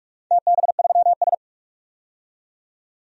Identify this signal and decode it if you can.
Morse code, heard as a keyed tone, about 46 wpm, 700 Hz. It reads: TB3S
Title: DXCC-100 Call Signs - 4 Characters - 45wpm